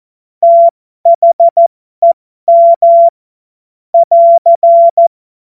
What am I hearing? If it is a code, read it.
Morse code, heard as a keyed tone, about 14 wpm, spaced as for 12 wpm, 680 Hz. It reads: THEM <AR>
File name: Morsekurs / Witze